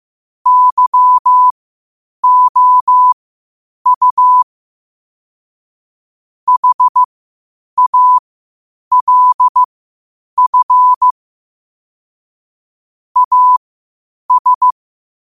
Morse code, heard as a keyed tone, about 15 wpm, spaced as for 8 wpm, 1000 Hz. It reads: YOU HALF AS
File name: qotdfiles / 8wpm